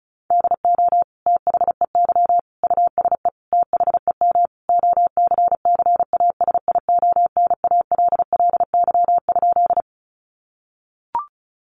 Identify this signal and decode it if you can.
Morse code, heard as a keyed tone, about 34 wpm, 700 Hz. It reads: DO THEY USE THEM OCCASIONALLY?
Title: Sentences from Top 1300 Words 35wpm